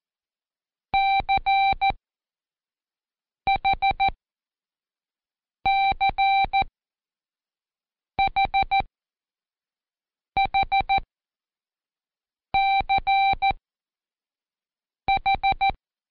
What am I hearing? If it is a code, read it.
Morse code, heard as a keyed tone, about 14 wpm, spaced as for 5 wpm, 780 Hz. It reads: CHCHHCH
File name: Code Quick / AudioDownld1 / Seg33-Day4-Pract H